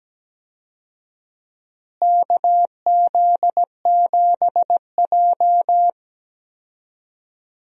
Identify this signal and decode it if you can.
Morse code, heard as a keyed tone, about 17 wpm, 700 Hz. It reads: KZ7J
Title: USA Call Signs - 4 Characters - Repeated 3x - 17wpm